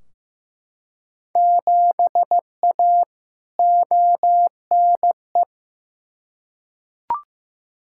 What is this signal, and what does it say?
Morse code, heard as a keyed tone, about 15 wpm, 700 Hz. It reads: 7A ONE
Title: Field Day Exchanges Encore - 15wpm